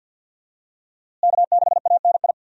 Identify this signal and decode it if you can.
Morse code, heard as a keyed tone, about 49 wpm, 700 Hz. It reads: K6ANI
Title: USA Call Signs - 5 Characters - Repeated 3x - 50wpm